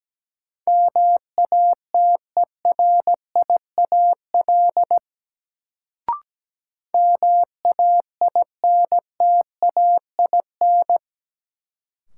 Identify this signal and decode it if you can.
Morse code, heard as a keyed tone, about 17 wpm, 700 Hz. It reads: MATERIAL MAINTAIN
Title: Sentences from Top 1000 Words - Review Part 5 - New Word Drill - 17wpm